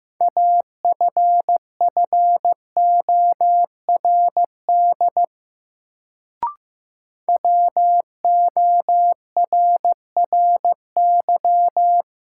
Morse code, heard as a keyed tone, about 15 wpm, 700 Hz. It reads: AFFORD WORRY
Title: Sentences from Top 1100 Words - New Word Drill - 15wpm